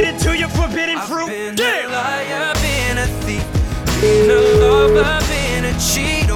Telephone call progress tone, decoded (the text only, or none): ringback tone